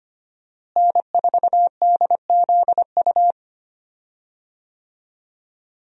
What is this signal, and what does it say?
Morse code, heard as a keyed tone, about 25 wpm, 700 Hz. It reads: N4DZU
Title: USA Call Signs - 5 Characters - Repeated 2x - 25wpm